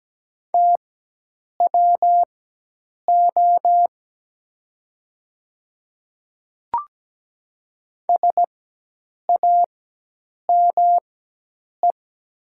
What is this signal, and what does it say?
Morse code, heard as a keyed tone, about 17 wpm, spaced as for 8 wpm, 700 Hz. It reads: TWO SAME
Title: 4x Character Spacing - Top 300 Words - 17wpm